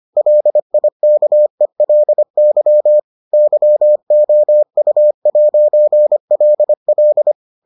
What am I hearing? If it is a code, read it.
Morse code, heard as a keyed tone, about 25 wpm, 600 Hz. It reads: LIKELY YOU'LL